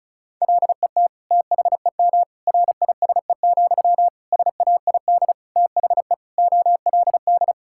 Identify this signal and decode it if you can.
Morse code, heard as a keyed tone, about 36 wpm, 700 Hz. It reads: LET THEM RISE, SAID THE OLD
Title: Sentences from Top 1000 Words - Review Part 6 - 35wpm